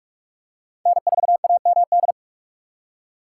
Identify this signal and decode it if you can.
Morse code, heard as a keyed tone, about 43 wpm, 700 Hz. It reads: N4AMD